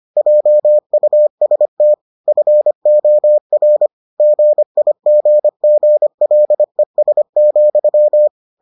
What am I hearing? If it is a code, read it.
Morse code, heard as a keyed tone, about 25 wpm, 600 Hz. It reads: JUST FOR GIGGLES,